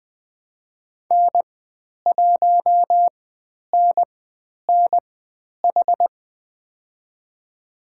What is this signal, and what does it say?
Morse code, heard as a keyed tone, about 20 wpm, spaced as for 10 wpm, 700 Hz. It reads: N1NNH
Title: 109 Lesson 18g - Letter m - Call Signs 20wpm